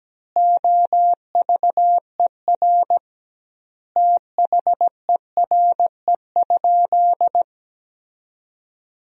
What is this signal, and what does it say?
Morse code, heard as a keyed tone, about 17 wpm, 700 Hz. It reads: OVER THERE?
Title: 2x Word-Spacing - Sentences from Top 200 Words - 17wpm